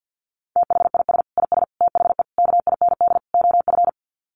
Morse code, heard as a keyed tone, about 50 wpm, 700 Hz. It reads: THIS IS THE KIND OF
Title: Sets of 5 Words Encore 50wpm